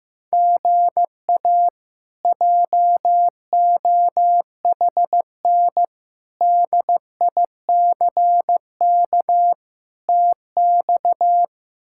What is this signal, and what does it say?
Morse code, heard as a keyed tone, about 15 wpm, 700 Hz. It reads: GA JOHN DICK TX
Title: K1USN SST QSO Practice - 15wpm